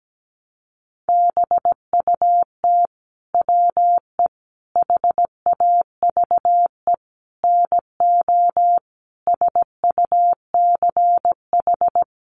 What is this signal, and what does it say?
Morse code, heard as a keyed tone, about 17 wpm, 700 Hz. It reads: BUT WE HAVE NO SUCH
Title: Sentences from Top 1300 Words 17wpm